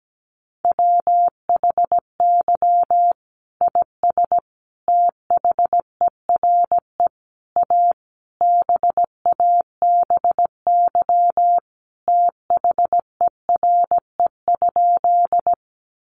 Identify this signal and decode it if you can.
Morse code, heard as a keyed tone, about 17 wpm, 700 Hz. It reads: WHY IS THERE A BABY THERE?
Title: Sentences from Top 1300 Words 17wpm